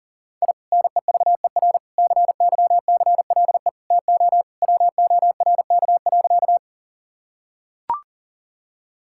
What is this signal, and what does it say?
Morse code, heard as a keyed tone, about 37 wpm, 690 Hz. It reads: I NEVER CYCLE TO WORK.
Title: Sentences from Top 1000 Words - Review Part 6 - 40wpm